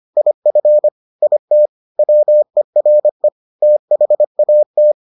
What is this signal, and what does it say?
Morse code, heard as a keyed tone, about 25 wpm, 600 Hz. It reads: IF IT WERE THAT